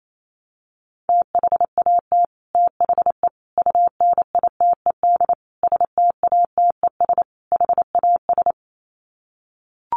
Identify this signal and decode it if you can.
Morse code, heard as a keyed tone, about 28 wpm, 700 Hz. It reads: THAT THE UNITED STATES HAS
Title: Sets of 5 Words Encore 28wpm